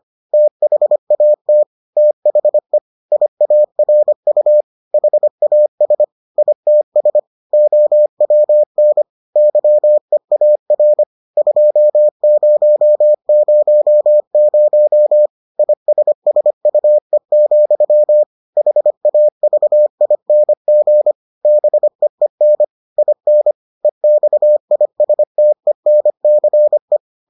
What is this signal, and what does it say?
Morse code, heard as a keyed tone, about 25 wpm, 600 Hz. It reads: THAT THE IARU HAS ITS OWN YEAR 2000 ISSUE, HAVING BEEN IN EXISTENCE